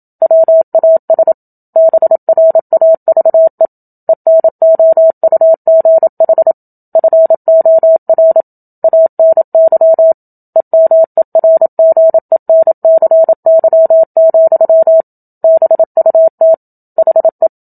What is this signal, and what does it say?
Morse code, heard as a keyed tone, about 27 wpm, 650 Hz. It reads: WAS BRAVE ENOUGH FOR ANY EMERGENCY, BUT HE
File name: Book Two / 01-Chapter 1 - 27 WPM